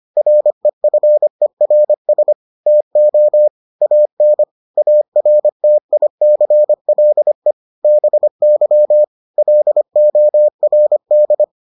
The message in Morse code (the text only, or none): REFERS TO AN ARTICLE BY LORD